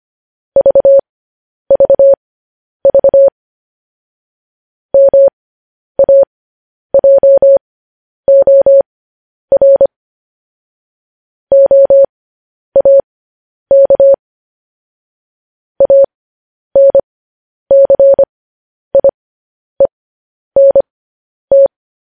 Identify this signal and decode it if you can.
Morse code, heard as a keyed tone, about 25 wpm, spaced as for 10 wpm, 550 Hz. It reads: VVV MAJOR OAK ANCIENT
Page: news-headlines-in-morse-code-at-10-wpm